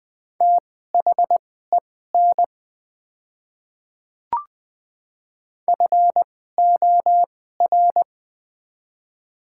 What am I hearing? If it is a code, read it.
Morse code, heard as a keyed tone, about 20 wpm, spaced as for 15 wpm, 700 Hz. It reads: THEN FOR
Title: 2x Character Spacing - Top 100 Words - 20wpm